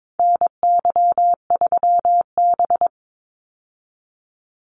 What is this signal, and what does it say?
Morse code, heard as a keyed tone, about 22 wpm, 700 Hz. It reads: NY3B
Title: Call Signs Repeated 3x - 22wpm